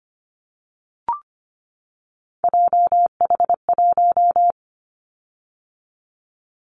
Morse code, heard as a keyed tone, about 25 wpm, 700 Hz. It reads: JH1
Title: Callsign Prefixes for Top 25 DXCC - 25wpm